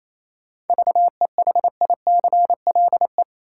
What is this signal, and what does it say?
Morse code, heard as a keyed tone, about 28 wpm, 700 Hz. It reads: VEHICLE
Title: Seven Letter Words - 28wpm